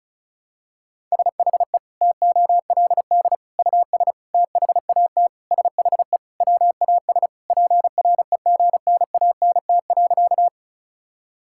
Morse code, heard as a keyed tone, about 36 wpm, 700 Hz. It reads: SHE TOLD US THAT SHE WAS PREGNANT.
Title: Sentences from Top 1000 Words - Review Part 6 - 35wpm